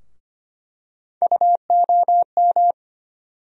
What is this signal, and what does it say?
Morse code, heard as a keyed tone, about 25 wpm, 700 Hz. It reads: UOM
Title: USA Call Sign Postfixes - 3 Letters 25wpm